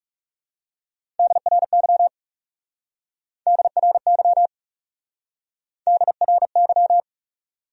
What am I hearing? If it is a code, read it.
Morse code, heard as a keyed tone, about 37 wpm, 700 Hz. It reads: DRY DRY DRY
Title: Speed-Racing Top 500 Words 30wpm